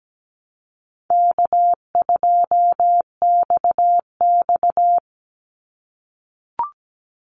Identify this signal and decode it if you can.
Morse code, heard as a keyed tone, about 17 wpm, 700 Hz. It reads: K2XX
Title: Easy Call Signs - 17wpm